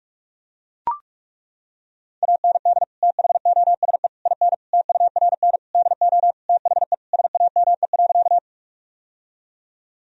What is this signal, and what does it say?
Morse code, heard as a keyed tone, about 41 wpm, 700 Hz. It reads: AND THOSE IN TURN DO THE SAME.
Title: Sentences from Top 1000 Words - Review Part 1 - 45wpm